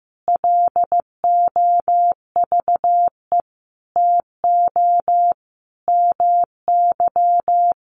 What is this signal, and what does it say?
Morse code, heard as a keyed tone, about 15 wpm, 700 Hz. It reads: LOVE TO MY
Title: Sentences from Top 1500 Words 15wpm